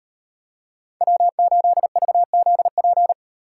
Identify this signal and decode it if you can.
Morse code, heard as a keyed tone, about 36 wpm, 700 Hz. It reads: W8VZP